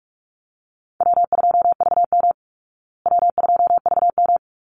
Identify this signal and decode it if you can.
Morse code, heard as a keyed tone, about 44 wpm, 700 Hz. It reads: W2VM W2VM
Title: Easy Call Signs Repeated 3x - 45wpm